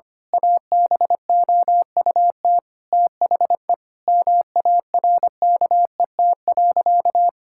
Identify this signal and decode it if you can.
Morse code, heard as a keyed tone, about 25 wpm, 700 Hz. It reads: ABOUT THE MARKET.
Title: Sentences from Top 1000 Words - Review Part 5 - 25wpm